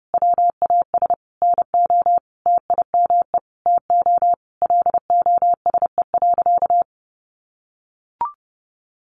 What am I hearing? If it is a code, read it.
Morse code, heard as a keyed tone, about 30 wpm, 700 Hz. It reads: WAS NO TIME TO LOSE.
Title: Sentences from Top 1100 Words 30wpm